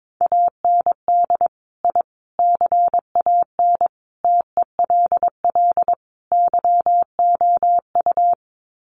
Morse code, heard as a keyed tone, about 22 wpm, 700 Hz. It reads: AND I CAN TELL YOU
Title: Sets of 5 Words Encore2 22wpm